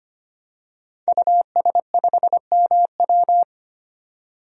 Morse code, heard as a keyed tone, about 25 wpm, 700 Hz. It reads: US5MW
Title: DXCC-100 Call Signs - 5 Characters - 25wpm